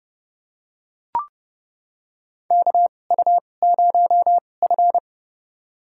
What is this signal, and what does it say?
Morse code, heard as a keyed tone, about 31 wpm, spaced as for 22 wpm, 700 Hz. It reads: KU0F
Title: USA Call Signs - 4 Characters - 2x Character Spacing - 30wpm